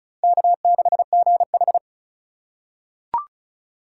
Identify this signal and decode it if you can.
Morse code, heard as a keyed tone, about 36 wpm, 700 Hz. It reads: K6GH
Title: USA Call Signs - 4x Character Spacing - 35wpm